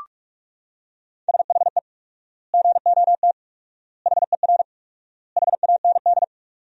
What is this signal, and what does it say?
Morse code, heard as a keyed tone, about 41 wpm, 700 Hz. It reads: SHE GOT HER HAND
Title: 4x Word-Spacing - Sentences from Top 500 Words - 45wpm